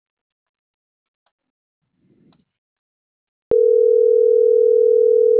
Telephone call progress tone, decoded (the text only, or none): ringback tone